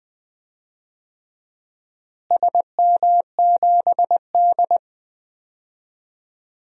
Morse code, heard as a keyed tone, about 21 wpm, 700 Hz. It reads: SM7D
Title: DXCC-100 Call Signs - 4 Characters - Repeated 2x - 20wpm